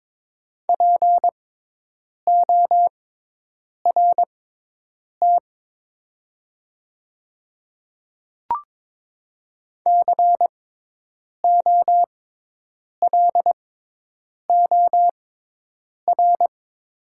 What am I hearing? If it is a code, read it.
Morse code, heard as a keyed tone, about 22 wpm, spaced as for 8 wpm, 700 Hz. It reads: PORT COLOR
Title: 6x Character Spacing - Top 500 Words - 22wpm